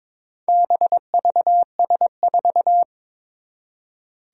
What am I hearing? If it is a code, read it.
Morse code, heard as a keyed tone, about 22 wpm, 700 Hz. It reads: BVS4
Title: Four Challenging Letter-Number Combinations 22wpm